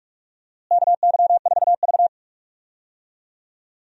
Morse code, heard as a keyed tone, about 45 wpm, 700 Hz. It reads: KY4V
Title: USA Call Signs - 4 Characters - Repeated 2x - 45wpm